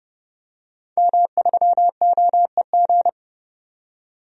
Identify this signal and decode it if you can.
Morse code, heard as a keyed tone, about 32 wpm, 700 Hz. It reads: M3OEG
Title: DXCC-100 Call Signs - 5 Characters - 30wpm